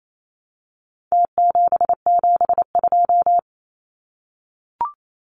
Morse code, heard as a keyed tone, about 28 wpm, 700 Hz. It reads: T772
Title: Numbers - Contest Style - 4-Digit 28wpm